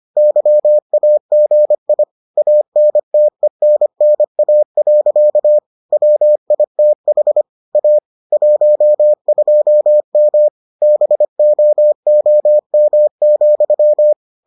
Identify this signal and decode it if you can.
Morse code, heard as a keyed tone, about 25 wpm, 600 Hz. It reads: YAGI ANTENNA. WITH A 12M BOOM,